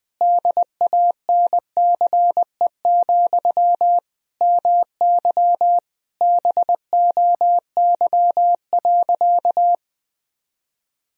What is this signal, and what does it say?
Morse code, heard as a keyed tone, about 20 wpm, 700 Hz. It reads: DANCE, MY BOY.
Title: Sentences from Top 1000 Words - Review Part 4 - 20wpm